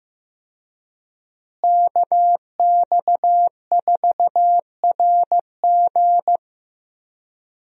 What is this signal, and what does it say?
Morse code, heard as a keyed tone, about 15 wpm, 700 Hz. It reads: KX4RG
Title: USA Call Signs - 5 Characters - Repeated 2x - 15wpm